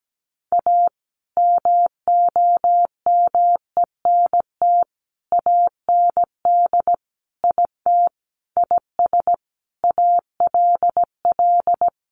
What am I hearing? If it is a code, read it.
Morse code, heard as a keyed tone, about 17 wpm, 700 Hz. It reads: A MOMENT AND IT IS ALL
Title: Sentences from Top 1000 Words 17wpm